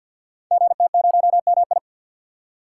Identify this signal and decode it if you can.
Morse code, heard as a keyed tone, about 46 wpm, 700 Hz. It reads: GT0MI
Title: Code Groups - 5 Letter:Numbers - 50wpm